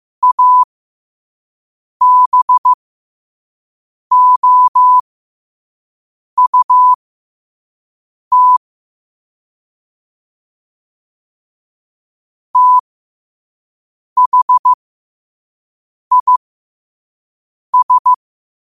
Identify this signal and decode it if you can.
Morse code, heard as a keyed tone, about 15 wpm, spaced as for 5 wpm, 1000 Hz. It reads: ABOUT THIS
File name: qotdfiles / 5wpm